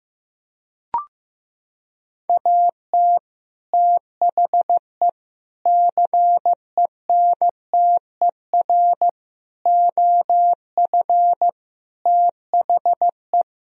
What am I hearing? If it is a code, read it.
Morse code, heard as a keyed tone, about 15 wpm, 700 Hz. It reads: AT THE CENTER OF THE
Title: Sentences from Top 1000 Words - Review Part 2 - 15wpm